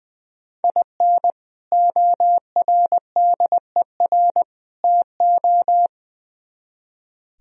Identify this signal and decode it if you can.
Morse code, heard as a keyed tone, about 22 wpm, 700 Hz. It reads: IN ORDER TO
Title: Speed-Racing - Sets of 3 Words 20wpm